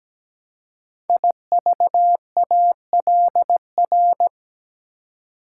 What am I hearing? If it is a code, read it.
Morse code, heard as a keyed tone, about 17 wpm, 700 Hz. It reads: IVALR
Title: Code Groups - 5 Letters - 17wpm